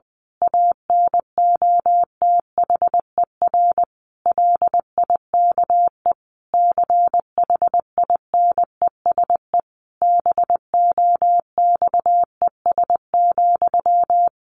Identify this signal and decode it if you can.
Morse code, heard as a keyed tone, about 20 wpm, 700 Hz. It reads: ANOTHER LIKE CHINESE BOXES,
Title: A Sound Of Thunder 20wpm